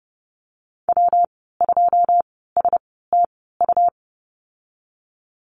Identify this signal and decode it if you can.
Morse code, heard as a keyed tone, about 30 wpm, spaced as for 17 wpm, 700 Hz. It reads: W2STU
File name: Lesson 19g - Letter w - Call Signs 30wpm